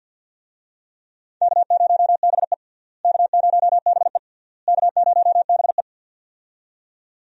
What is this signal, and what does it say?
Morse code, heard as a keyed tone, about 53 wpm, 700 Hz. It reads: K0BE K0BE K0BE
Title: USA Call Signs - 4 Characters - Repeated 3x - 50wpm